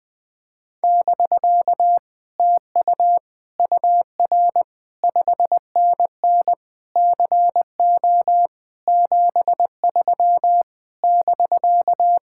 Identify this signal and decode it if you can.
Morse code, heard as a keyed tone, about 20 wpm, 700 Hz. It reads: <BK> TU UR 5NN CO 73 <BK>
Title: Speed-Racing Common QSO Phrases - 17wpm